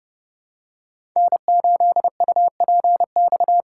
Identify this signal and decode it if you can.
Morse code, heard as a keyed tone, about 30 wpm, 700 Hz. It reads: N8UPX
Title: USA Call Signs - 5 Characters - 2x Character Spacing - 30wpm